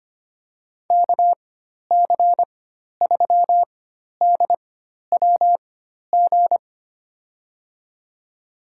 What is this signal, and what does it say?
Morse code, heard as a keyed tone, about 25 wpm, spaced as for 12 wpm, 700 Hz. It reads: KC3DWG
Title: USA Call Signs - 6 Characters - 4x Character Spacing - 25wpm